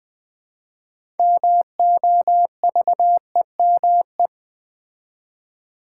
Morse code, heard as a keyed tone, about 20 wpm, 700 Hz. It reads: MOVEME
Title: Build A Word - 2 Letters At A Time - 8-Letter Words - 20wpm